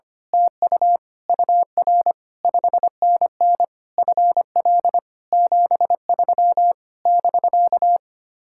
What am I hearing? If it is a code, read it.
Morse code, heard as a keyed tone, about 24 wpm, 700 Hz. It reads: TU UR 5NN FL 73 <BK>
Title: Speed-Racing Common QSO Phrases - 25wpm